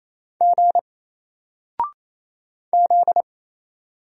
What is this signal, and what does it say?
Morse code, heard as a keyed tone, about 28 wpm, 700 Hz. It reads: G Z
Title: Single Character (Letter:Number:Punctuation:Prosign) 28wpm